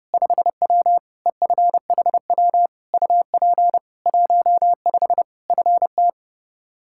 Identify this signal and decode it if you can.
Morse code, heard as a keyed tone, about 30 wpm, 700 Hz. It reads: 5W EFHW UP 15 FT
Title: Practice QSO - Exchange 2 - Info - 30wpm